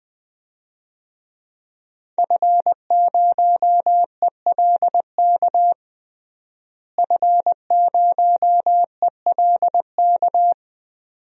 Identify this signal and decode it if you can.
Morse code, heard as a keyed tone, about 20 wpm, 700 Hz. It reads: F0ELK F0ELK
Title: DXCC-100 Call Signs - 5 Characters - Repeated 2x - 20wpm